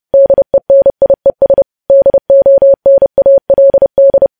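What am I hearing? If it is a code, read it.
Morse code, heard as a keyed tone, about 30 wpm, 550 Hz. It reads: DENIES DONALD
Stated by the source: News headlines sent in Morse at 30 wpm, preceded by VVV.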